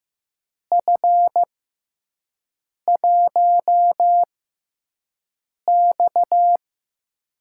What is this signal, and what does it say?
Morse code, heard as a keyed tone, about 15 wpm, spaced as for 5 wpm, 700 Hz. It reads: F1X
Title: USA Call Signs - 4 Characters - 6x Character Spacing - 15wpm